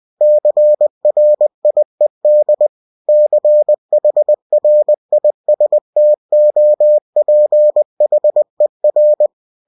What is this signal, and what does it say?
Morse code, heard as a keyed tone, about 20 wpm, 600 Hz. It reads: CRIED CHRISTOPHER
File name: cw-books / winnie-the-pooh / winnie-20wpm-page-0098